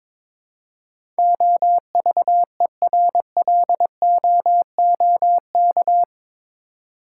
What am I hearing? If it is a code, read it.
Morse code, heard as a keyed tone, about 22 wpm, 700 Hz. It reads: OVERLOOK
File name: practice / 22wpm / Eight Letter Words - 22wpm